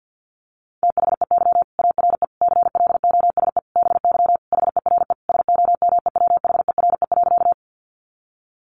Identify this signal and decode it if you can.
Morse code, heard as a keyed tone, about 50 wpm, 700 Hz. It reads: THEY ARE CLOSE BY HERE SOMEWHERE.
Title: Sentences from Top 1400 Words 50wpm